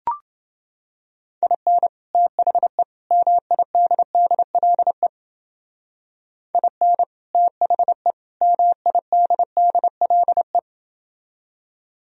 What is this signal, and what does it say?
Morse code, heard as a keyed tone, about 30 wpm, 700 Hz. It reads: IN THE MIDDLE IN THE MIDDLE